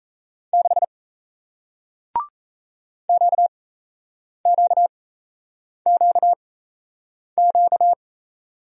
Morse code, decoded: 6 QQQQ

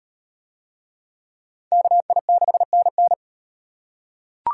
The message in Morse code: KI6NN

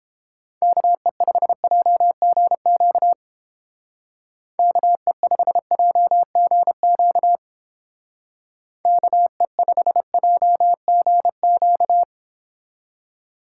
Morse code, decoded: KE5JGQ KE5JGQ KE5JGQ

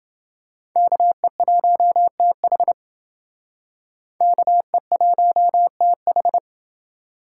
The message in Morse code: KE1TH KE1TH

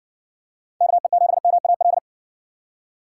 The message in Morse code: KE7MAL

